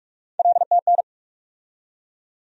Morse code, heard as a keyed tone, about 41 wpm, 700 Hz. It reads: LTN
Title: USA Call Sign Postfixes - 3 Letters 45wpm